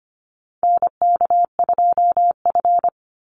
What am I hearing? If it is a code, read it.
Morse code, heard as a keyed tone, about 24 wpm, 700 Hz. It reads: NK2F
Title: Call Signs Repeated 3x - 25wpm